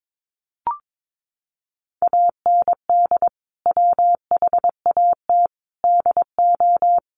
Morse code, heard as a keyed tone, about 22 wpm, 700 Hz. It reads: AND WHAT DO
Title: Sentences from Top 100 Words 22wpm